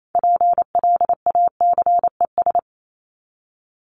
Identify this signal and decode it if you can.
Morse code, heard as a keyed tone, about 28 wpm, 700 Hz. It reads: PLACES
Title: ES:S Suffix - Top 100 Words with Base Words 28wpm